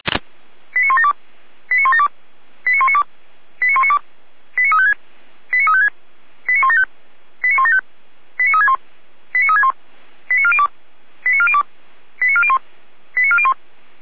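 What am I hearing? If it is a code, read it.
Five-tone selective call (ZVEI), heard as quick runs of five tones, tones of about 70 ms. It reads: 89172 89172 89192 89192 89356 89356 89175 89175 89361 89361 89402 89402 89401 89401
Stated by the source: ZVEI five-tone selective call 89370 received on an SDR.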